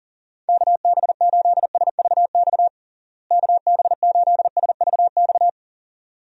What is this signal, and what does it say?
Morse code, heard as a keyed tone, about 40 wpm, 700 Hz. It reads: KB8SVX KB8SVX
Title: USA Call Signs - 6 Characters - Repeated 2x - 40wpm